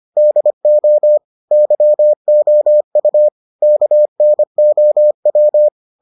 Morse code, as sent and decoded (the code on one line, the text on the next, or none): -.. --- / -.-- --- ..- / -.- -. --- .--
DO YOU KNOW